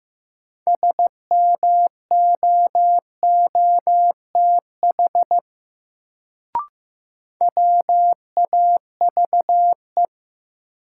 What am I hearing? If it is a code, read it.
Morse code, heard as a keyed tone, about 15 wpm, 700 Hz. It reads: SMOOTH WAVE
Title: Sentences from Top 1400 Words - New Word Drill - 15wpm